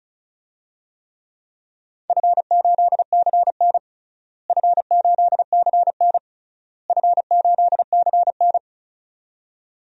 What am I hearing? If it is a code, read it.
Morse code, heard as a keyed tone, about 34 wpm, 700 Hz. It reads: F8CN F8CN F8CN
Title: DXCC-100 Call Signs - 4 Characters - Repeated 3x - 35wpm